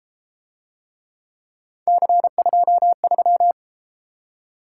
Morse code, heard as a keyed tone, about 30 wpm, 700 Hz. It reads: C23